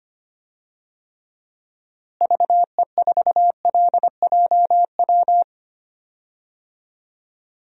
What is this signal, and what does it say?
Morse code, heard as a keyed tone, about 25 wpm, 700 Hz. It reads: VE4LJW
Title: DXCC-100 Call Signs - 6 Characters - Repeated 2x - 25wpm